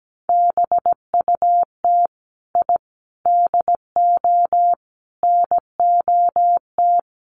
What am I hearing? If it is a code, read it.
Morse code, heard as a keyed tone, about 17 wpm, 700 Hz. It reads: BUT I DO NOT
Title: Sets of 5 Words Encore 17wpm